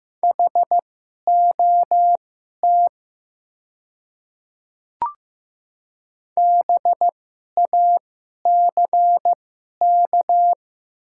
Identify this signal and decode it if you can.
Morse code, heard as a keyed tone, about 15 wpm, spaced as for 11 wpm, 700 Hz. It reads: HOT BACK
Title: 2x Character Spacing - Top 300 Words - 15wpm